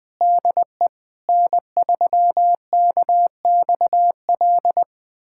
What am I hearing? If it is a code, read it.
Morse code, heard as a keyed tone, about 21 wpm, 700 Hz. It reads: DE N3KXL
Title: USA Callsign DE Callsign - 5 Characters - 20wpm